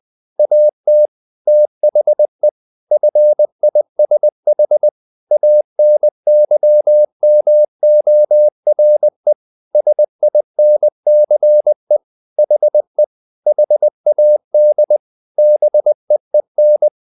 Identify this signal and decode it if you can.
Morse code, heard as a keyed tone, about 20 wpm, 600 Hz. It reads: AT THE FISH ANYMORE SINCE HE HAD BEEN
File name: the old man and the sea part 2 20wpm